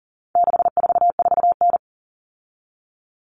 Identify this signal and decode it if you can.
Morse code, heard as a keyed tone, about 40 wpm, 700 Hz. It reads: 644N
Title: All Cut Numbers - 4-Digit 40wpm